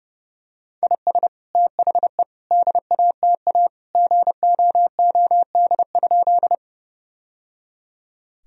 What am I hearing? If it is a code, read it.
Morse code, heard as a keyed tone, about 28 wpm, 700 Hz. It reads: IS THE DATA GOOD?
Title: Sentences from Top 1000 Words - Review Part 3 - 30wpm